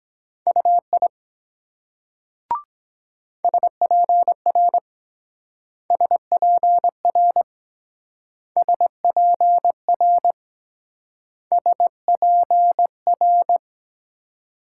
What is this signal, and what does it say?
Morse code, heard as a keyed tone, about 23 wpm, 700 Hz. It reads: UI SPR SPR SPR SPR